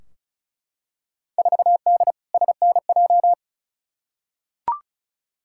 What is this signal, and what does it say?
Morse code, heard as a keyed tone, about 34 wpm, 700 Hz. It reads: 4D SNJ
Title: Field Day Exchanges Encore - 35wpm